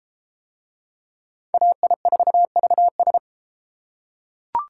Morse code, heard as a keyed tone, about 32 wpm, 700 Hz. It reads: AI4VS